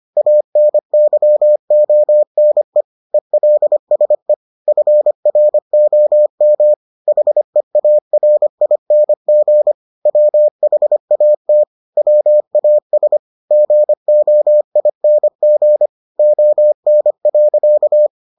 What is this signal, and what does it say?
Morse code, heard as a keyed tone, about 25 wpm, 600 Hz. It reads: ANYONE ELSE FROM HEARING WHAT WAS GOING ON.